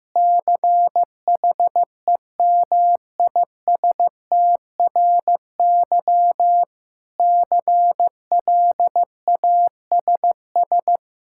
Morse code, heard as a keyed tone, about 15 wpm, 700 Hz. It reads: CHEMISTRY CLASS
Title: Sentences from Top 1000 Words - Review Part 6 - 15wpm